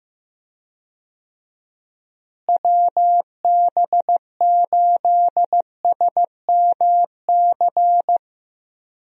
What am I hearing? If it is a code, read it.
Morse code, heard as a keyed tone, about 15 wpm, 700 Hz. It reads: WB8SMC
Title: USA Call Signs - 6 Characters - Repeated 2x - 15wpm